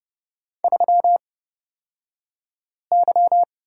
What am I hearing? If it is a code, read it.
Morse code, heard as a keyed tone, about 30 wpm, spaced as for 5 wpm, 700 Hz. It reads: 3Y
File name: Lesson 37d - Letter x - Two Characters 30wpm